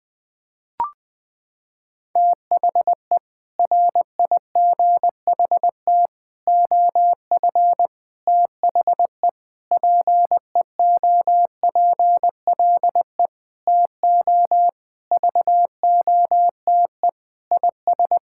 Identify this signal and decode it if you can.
Morse code, heard as a keyed tone, about 20 wpm, 700 Hz. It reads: THE RIGHT OF THE PEOPLE TO VOTE IS
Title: Sentences with 6-Word Phrases - 20wpm